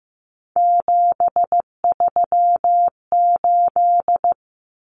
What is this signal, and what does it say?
Morse code, heard as a keyed tone, about 15 wpm, 700 Hz. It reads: 738
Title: Numbers - 3-Digit 15wpm